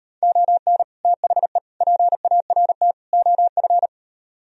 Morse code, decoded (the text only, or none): ON THE PART OF